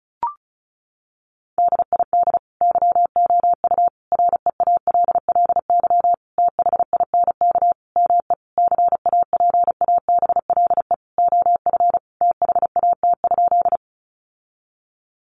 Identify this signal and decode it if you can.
Morse code, heard as a keyed tone, about 34 wpm, 700 Hz. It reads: DID YOU REALLY THINK ME CAPABLE OF THAT?
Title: Sentences from Top 1200 Words 35wpm